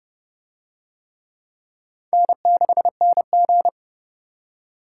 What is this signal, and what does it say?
Morse code, heard as a keyed tone, about 26 wpm, 700 Hz. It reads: N6NG